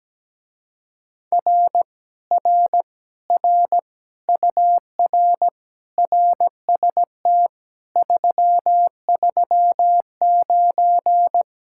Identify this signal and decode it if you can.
Morse code, heard as a keyed tone, about 17 wpm, 700 Hz. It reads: R R R UR RST 339
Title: Speed-Racing Common QSO Phrases - 17wpm